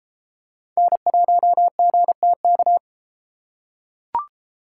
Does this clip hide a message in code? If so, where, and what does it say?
Yes: Morse code from 0.8 to 2.8 s, heard as a keyed tone, about 31 wpm, 700 Hz. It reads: N1GTK